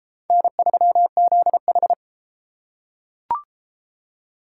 Morse code, heard as a keyed tone, about 29 wpm, 700 Hz. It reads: N3ZH